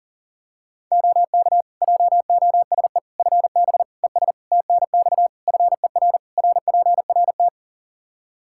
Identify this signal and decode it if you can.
Morse code, heard as a keyed tone, about 40 wpm, 700 Hz. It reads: OK JOSE FB ES TNX FER RPRT
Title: K1USN SST QSO Practice 40wpm